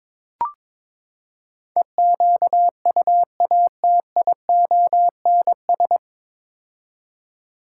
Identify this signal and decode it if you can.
Morse code, heard as a keyed tone, about 22 wpm, 700 Hz. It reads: EQUATIONS
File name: practice / 22wpm / Build A Phrase - 5-Word Phrases - 22wpm